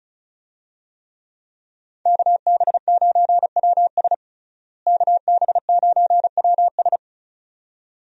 Morse code, heard as a keyed tone, about 36 wpm, 700 Hz. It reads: KB9WS KB9WS
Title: USA Call Signs - 5 Characters - Repeated 2x - 35wpm